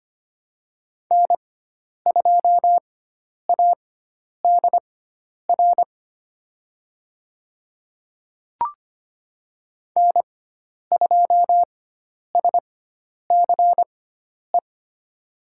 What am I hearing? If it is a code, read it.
Morse code, heard as a keyed tone, about 25 wpm, spaced as for 10 wpm, 700 Hz. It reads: N2ADR N2SCE
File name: Lesson 16g - Letter c - Call Signs 25wpm